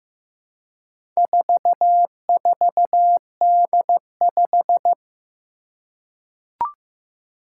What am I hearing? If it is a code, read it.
Morse code, heard as a keyed tone, about 15 wpm, 700 Hz. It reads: 44D5
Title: Four Challenging Letter-Number Combinations 15wpm